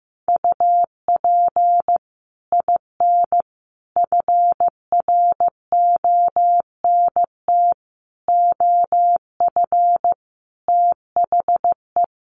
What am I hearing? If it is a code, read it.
Morse code, heard as a keyed tone, about 15 wpm, 700 Hz. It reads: UP IN FRONT OF THE
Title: Sets of 5 Words Encore2 15wpm